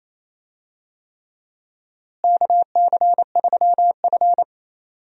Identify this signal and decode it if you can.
Morse code, heard as a keyed tone, about 28 wpm, 700 Hz. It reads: KC3F
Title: USA Call Signs - 4 Characters - Repeated 3x - 28wpm